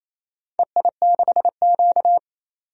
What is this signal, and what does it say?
Morse code, heard as a keyed tone, about 28 wpm, 700 Hz. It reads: EI6Q
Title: DXCC-100 Call Signs - 4 Characters - 28wpm